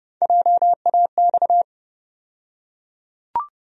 Morse code, heard as a keyed tone, about 30 wpm, 700 Hz. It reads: JAX
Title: USA Call Sign Postfixes - 3 Letters 30wpm